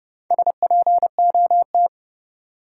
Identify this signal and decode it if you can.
Morse code, heard as a keyed tone, about 32 wpm, 700 Hz. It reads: SPOT
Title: Sentences from Top 1100 Words - New Word Drill Alt - 30wpm